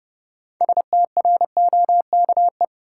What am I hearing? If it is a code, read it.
Morse code, heard as a keyed tone, about 30 wpm, 700 Hz. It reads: STROKE
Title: Six Letter Words - 30wpm